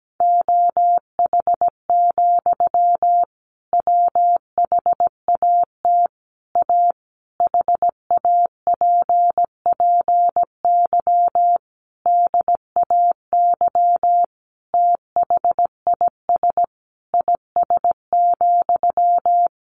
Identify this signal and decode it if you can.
Morse code, heard as a keyed tone, about 17 wpm, 700 Hz. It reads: OH, WHAT A HAPPY DAY THIS IS,
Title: Sentences from Top 1500 Words 17wpm